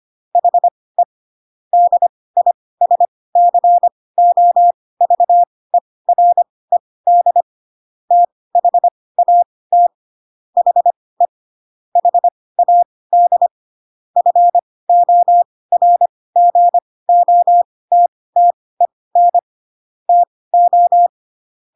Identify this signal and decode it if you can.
Morse code, heard as a keyed tone, about 25 wpm, spaced as for 18 wpm, 700 Hz. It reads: HE DISCOVERED THAT HE HAD FORGOTTEN TO
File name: Pinocchio ch3 12 25 ik1hge